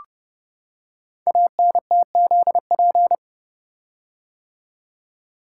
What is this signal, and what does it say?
Morse code, heard as a keyed tone, about 30 wpm, 700 Hz. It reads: ANTZP
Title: Code Groups - 5 Letters - 30wpm